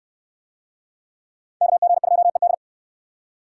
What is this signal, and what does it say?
Morse code, heard as a keyed tone, about 69 wpm, 700 Hz. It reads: KG2ED